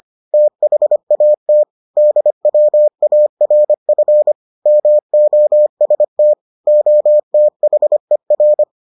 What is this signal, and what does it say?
Morse code, heard as a keyed tone, about 25 wpm, 600 Hz. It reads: THAT DWARF MOST OTHER